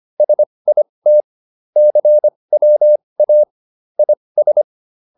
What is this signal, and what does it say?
Morse code, heard as a keyed tone, about 25 wpm, spaced as for 20 wpm, 600 Hz. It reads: SIT CWA IS